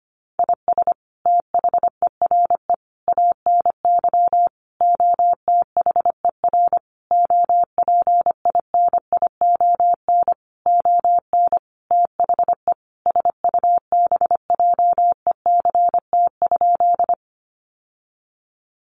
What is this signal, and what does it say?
Morse code, heard as a keyed tone, about 25 wpm, 700 Hz. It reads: IS THERE ANY OTHER OPINION ON THE SUBJECT?
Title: Sentences from Top 1200 Words 25wpm